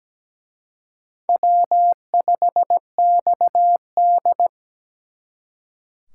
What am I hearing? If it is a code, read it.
Morse code, heard as a keyed tone, about 20 wpm, 700 Hz. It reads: W5XD
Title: Speed Racing - USA Call Signs 17wpm